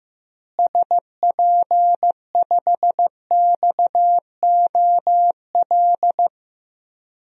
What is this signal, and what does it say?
Morse code, heard as a keyed tone, about 15 wpm, 700 Hz. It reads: SP5XOL
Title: DXCC-100 Call Signs - 6 Characters - Repeated 3x - 15wpm